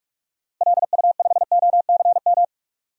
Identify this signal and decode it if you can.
Morse code, heard as a keyed tone, about 43 wpm, 690 Hz. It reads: LU5OCM
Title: DXCC-100 Call Signs - 6 Characters - Repeated 2x - 45wpm